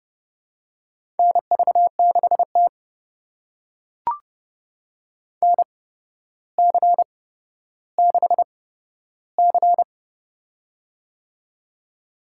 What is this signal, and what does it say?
Morse code, heard as a keyed tone, about 30 wpm, spaced as for 13 wpm, 700 Hz. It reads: NV6T NC6C
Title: USA Call Signs - 4 Characters - 8x Character Spacing - 30wpm